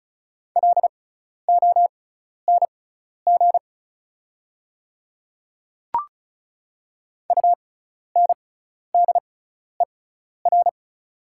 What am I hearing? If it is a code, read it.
Morse code, heard as a keyed tone, about 36 wpm, spaced as for 12 wpm, 700 Hz. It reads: LONG UNDER
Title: 6x Character Spacing - Top 200 Words - 35wpm